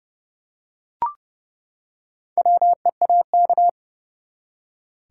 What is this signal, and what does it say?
Morse code, heard as a keyed tone, about 30 wpm, 700 Hz. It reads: WEAK
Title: Four Letter Words - 30wpm